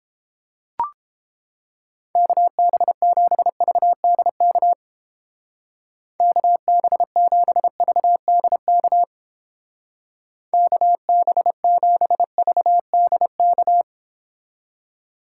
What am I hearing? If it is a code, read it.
Morse code, heard as a keyed tone, about 31 wpm, 700 Hz. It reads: KB7VDK KB7VDK KB7VDK